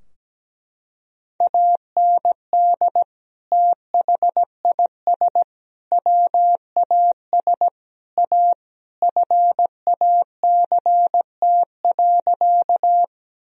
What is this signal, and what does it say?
Morse code, heard as a keyed tone, about 17 wpm, 700 Hz. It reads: AND THIS WAS A FACT.
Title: Sentences from Top 1000 Words - Review Part 3 - 17wpm